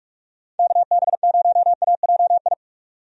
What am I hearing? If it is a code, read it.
Morse code, heard as a keyed tone, about 46 wpm, 700 Hz. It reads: KB0AJI
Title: USA Call Signs - 6 Characters - 45wpm